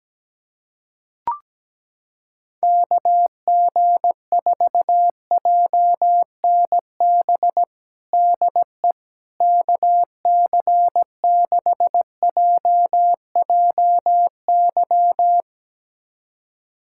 Morse code, heard as a keyed tone, about 17 wpm, 700 Hz. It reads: KG4JNB DE KC6JJY
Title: USA Callsign DE Callsign 17wpm